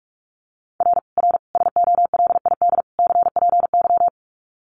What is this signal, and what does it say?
Morse code, heard as a keyed tone, about 44 wpm, 700 Hz. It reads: R R SOLID CPY
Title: Common QSO Phrases 45wpm